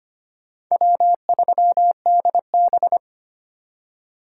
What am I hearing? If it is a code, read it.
Morse code, heard as a keyed tone, about 25 wpm, 700 Hz. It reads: W3DB
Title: USA Call Signs - 4 Characters - 2x Character Spacing - 25wpm